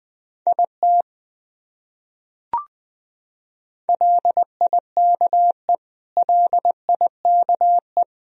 Morse code, heard as a keyed tone, about 19 wpm, 700 Hz. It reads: IT LIKE LIKE
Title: Top 100 Words Repeated 20wpm